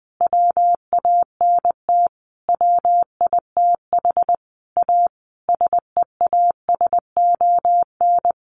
Morse code, heard as a keyed tone, about 19 wpm, 700 Hz. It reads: WANT WITH A SEASON
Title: Sentences from Top 1400 Words 20wpm